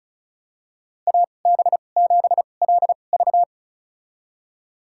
Morse code, heard as a keyed tone, about 36 wpm, spaced as for 25 wpm, 700 Hz. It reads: AB7LV
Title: USA Call Signs - 5 Characters - 2x Character Spacing - 35wpm